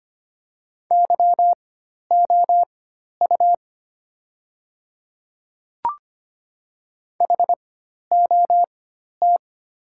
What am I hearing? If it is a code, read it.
Morse code, heard as a keyed tone, about 25 wpm, spaced as for 12 wpm, 700 Hz. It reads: YOU HOT
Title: 4x Character Spacing - Top 100 Words - 25wpm